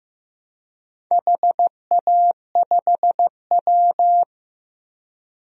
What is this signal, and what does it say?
Morse code, heard as a keyed tone, about 15 wpm, 700 Hz. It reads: HA5W
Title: DXCC-100 Call Signs - 4 Characters - 15wpm